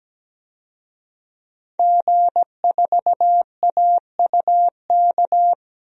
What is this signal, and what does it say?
Morse code, heard as a keyed tone, about 17 wpm, 700 Hz. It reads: G4AUK
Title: DXCC-100 Call Signs - 5 Characters - Repeated 3x - 17wpm